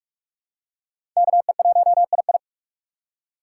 Morse code, heard as a keyed tone, about 42 wpm, 700 Hz. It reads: KE1II
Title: USA Call Signs - 5 Characters - 4x Character Spacing - 45wpm